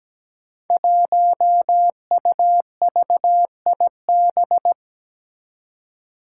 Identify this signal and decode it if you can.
Morse code, heard as a keyed tone, about 17 wpm, 700 Hz. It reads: 1UVIB
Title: Five Challenging Letter-Number Combinations 17wpm